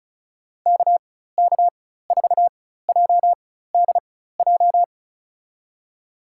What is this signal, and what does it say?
Morse code, heard as a keyed tone, about 34 wpm, spaced as for 16 wpm, 700 Hz. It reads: KK4JDJ
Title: USA Call Signs - 4x Character Spacing - 35wpm